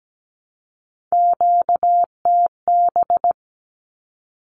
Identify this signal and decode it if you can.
Morse code, heard as a keyed tone, about 17 wpm, 700 Hz. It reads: QTB
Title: Q-Codes 17wpm